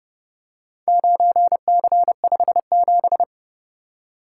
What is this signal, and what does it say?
Morse code, heard as a keyed tone, about 29 wpm, 700 Hz. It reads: 9C57